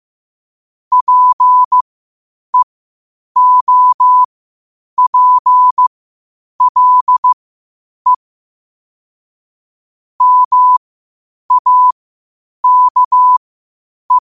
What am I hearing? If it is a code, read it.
Morse code, heard as a keyed tone, about 15 wpm, spaced as for 8 wpm, 1000 Hz. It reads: PEOPLE MAKE
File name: qotdfiles / 8wpm